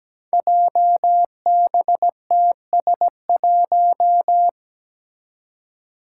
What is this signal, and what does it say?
Morse code, heard as a keyed tone, about 17 wpm, 700 Hz. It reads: JBTS1
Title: Five Challenging Letter-Number Combinations 17wpm